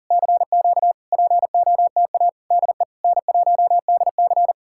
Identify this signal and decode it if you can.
Morse code, heard as a keyed tone, about 59 wpm, spaced as for 49 wpm, 700 Hz. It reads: CQ POTA DE N1DC
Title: Speed-Racing Common QSO Phrases - 40wpm